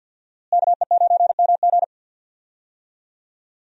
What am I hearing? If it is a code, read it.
Morse code, heard as a keyed tone, about 53 wpm, 700 Hz. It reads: KE9MG
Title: USA Call Signs - 5 Characters - Repeated 3x - 50wpm